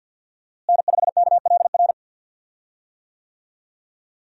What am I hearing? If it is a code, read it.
Morse code, heard as a keyed tone, about 51 wpm, 700 Hz. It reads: N5KLR
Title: USA Call Signs - 5 Characters - Repeated 3x - 50wpm